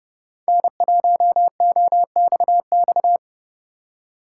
Morse code, heard as a keyed tone, about 30 wpm, 700 Hz. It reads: N1OXX